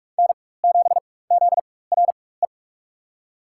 Morse code, heard as a keyed tone, about 42 wpm, spaced as for 21 wpm, 700 Hz. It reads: N7ZRE
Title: USA Call Signs - 5 Characters - 4x Character Spacing - 45wpm